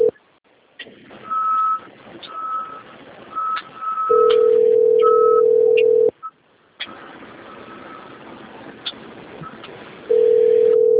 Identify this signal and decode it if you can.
Ringback tone (North America), 440 + 480 Hz.
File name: rg-1600-unknown- - - .37271